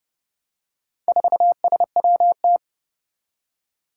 Morse code, heard as a keyed tone, about 32 wpm, 700 Hz. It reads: 4SWT
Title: Four Challenging Letter-Number Combinations 30wpm